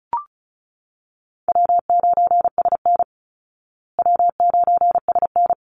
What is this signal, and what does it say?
Morse code, heard as a keyed tone, about 34 wpm, 700 Hz. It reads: W9SN W9SN
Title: Call Signs Repeated 3x - 35wpm